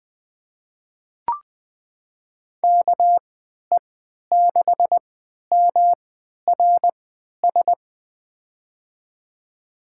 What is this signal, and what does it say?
Morse code, heard as a keyed tone, about 20 wpm, spaced as for 11 wpm, 700 Hz. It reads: KE6MRS
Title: USA Call Signs - 6 Characters - 3x Character Spacing - 20wpm